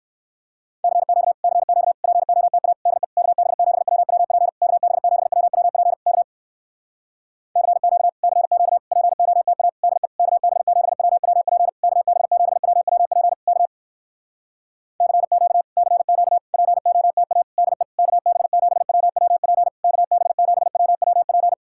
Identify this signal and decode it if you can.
Morse code, heard as a keyed tone, about 59 wpm, 700 Hz. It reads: CQ CQ POTA DE KB7WWP KB7WWP K CQ CQ POTA DE KB7WWP KB7WWP K CQ CQ POTA DE KB7WWP KB7WWP